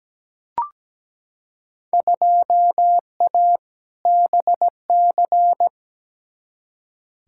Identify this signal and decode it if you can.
Morse code, heard as a keyed tone, about 17 wpm, 700 Hz. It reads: 2A BC
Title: Field Day Exchanges Encore - 17wpm